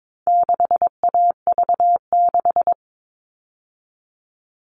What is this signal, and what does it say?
Morse code, heard as a keyed tone, about 22 wpm, 700 Hz. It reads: 6AV6
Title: All Cut Numbers - 4-Digit 22wpm